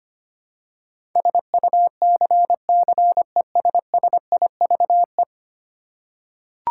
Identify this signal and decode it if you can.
Morse code, heard as a keyed tone, about 25 wpm, 700 Hz. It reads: SUCCESSIVE
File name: Ten Letter Words - 25wpm